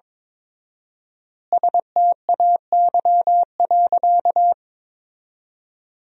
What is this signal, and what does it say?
Morse code, heard as a keyed tone, about 22 wpm, 700 Hz. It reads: STAY.
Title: 4x Word-Spacing - Sentences from Top 500 Words - 22wpm